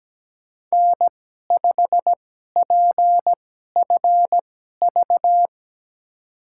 Morse code, heard as a keyed tone, about 17 wpm, spaced as for 12 wpm, 700 Hz. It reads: N5PFV
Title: USA Call Signs - 2x Character Spacing - 17wpm